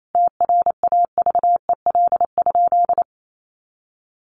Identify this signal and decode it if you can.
Morse code, heard as a keyed tone, about 28 wpm, 700 Hz. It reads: TRAVEL?
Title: Sentences from Top 1400 Words 28wpm